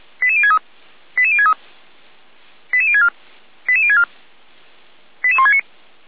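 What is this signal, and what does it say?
Five-tone selective call (ZVEI), heard as quick runs of five tones, tones of about 70 ms. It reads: 88063 88063 88064 88064 88169
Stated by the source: ZVEI five-tone selective call 88352 received on an SDR.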